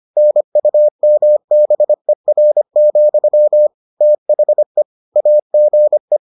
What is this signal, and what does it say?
Morse code, heard as a keyed tone, about 25 wpm, 600 Hz. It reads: NUMBER, THE AGE